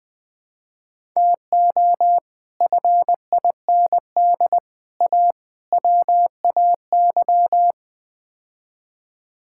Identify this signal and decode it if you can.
Morse code, heard as a keyed tone, about 21 wpm, 700 Hz. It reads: TO FIND A WAY
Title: Build A Phrase - 5-Word Phrases - 20wpm